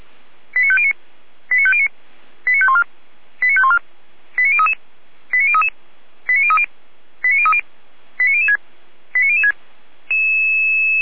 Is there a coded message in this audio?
Yes: five-tone selective call (ZVEI) from 0.5 to 9.5 s, heard as quick runs of five tones, tones of about 70 ms. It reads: 89509 89509 89514 89514 89033 89033 89030 89030 89006 89006